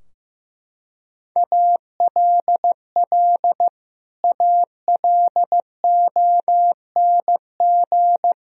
Morse code, heard as a keyed tone, about 15 wpm, 700 Hz. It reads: ALL ALONG
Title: Sentences from Top 1000 Words - Review Part 2 - 15wpm